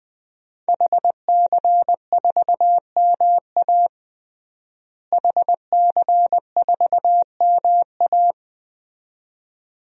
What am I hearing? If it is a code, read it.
Morse code, heard as a keyed tone, about 20 wpm, 700 Hz. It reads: HC4MA HC4MA
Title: DXCC-100 Call Signs - 5 Characters - Repeated 2x - 20wpm